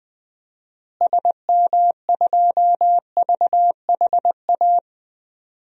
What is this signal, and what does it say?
Morse code, heard as a keyed tone, about 20 wpm, 700 Hz. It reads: SM2VHA
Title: DXCC-100 Call Signs - 6 Characters - Repeated 2x - 20wpm